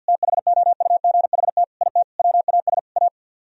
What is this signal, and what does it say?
Morse code, heard as a keyed tone, about 64 wpm, 700 Hz. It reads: THOUGHT IT WAS A